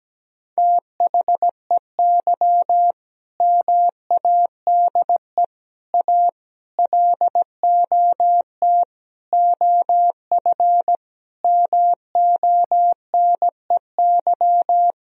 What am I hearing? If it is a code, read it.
Morse code, heard as a keyed tone, about 17 wpm, 700 Hz. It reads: THEY MADE A LOT OF MONEY
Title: Sentences from Top 1000 Words - Review Part 4 - 17wpm